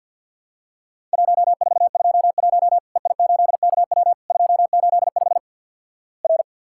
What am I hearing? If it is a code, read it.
Morse code, heard as a keyed tone, about 50 wpm, 700 Hz. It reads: 1421 EI8KW 285 R
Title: Worked All Europe DX Contest - QTC Exchange - 50wpm